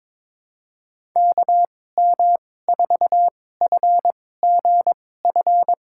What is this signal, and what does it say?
Morse code, heard as a keyed tone, about 22 wpm, spaced as for 16 wpm, 700 Hz. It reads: KM4FGF
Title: USA Call Signs - 6 Characters - 2x Character Spacing - 22wpm